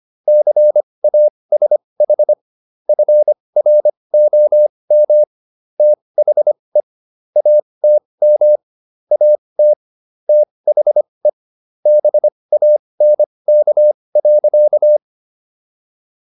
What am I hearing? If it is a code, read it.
Morse code, heard as a keyed tone, about 25 wpm, spaced as for 20 wpm, 600 Hz. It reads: CASH FROM THE ATM AT THE BANK.